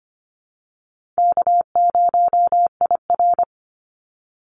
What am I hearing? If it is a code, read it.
Morse code, heard as a keyed tone, about 25 wpm, 700 Hz. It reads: K0IR
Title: Easy Call Signs Repeated 2x - 25wpm